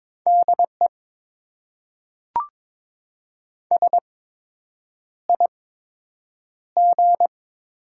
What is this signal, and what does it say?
Morse code, heard as a keyed tone, about 22 wpm, spaced as for 8 wpm, 700 Hz. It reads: DE SIG
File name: Top 90 QSO Elements - 8x Character Spacing - 22wpm